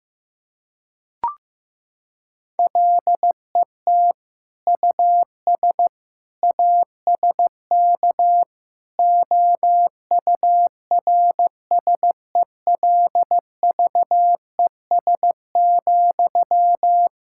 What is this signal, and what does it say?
Morse code, heard as a keyed tone, about 15 wpm, 700 Hz. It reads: LET US ASK OURSELVES,
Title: Sentences from Top 1600 Words 15wpm